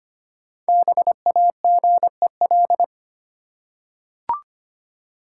Morse code, heard as a keyed tone, about 25 wpm, 700 Hz. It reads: BAGEL
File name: practice / 25wpm / Easy Words with Hints - 25wpm